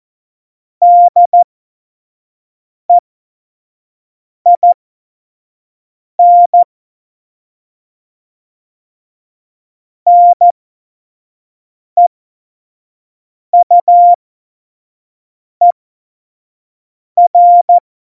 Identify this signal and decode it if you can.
Morse code, heard as a keyed tone, about 14 wpm, spaced as for 5 wpm, 700 Hz. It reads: DEIN NEUER
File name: Morsekurs / Witze / Witz 03. Oktober 17, Tempo 25 F70